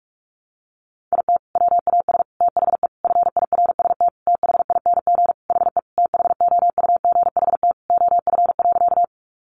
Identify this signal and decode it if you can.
Morse code, heard as a keyed tone, about 44 wpm, 700 Hz. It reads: IT WAS THE FIRST THING HE THOUGHT OF.
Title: Sentences from Top 300 Words 45wpm